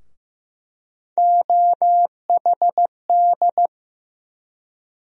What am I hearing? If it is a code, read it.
Morse code, heard as a keyed tone, about 15 wpm, 700 Hz. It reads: OHD
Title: USA Call Sign Postfixes - 3 Letters 15wpm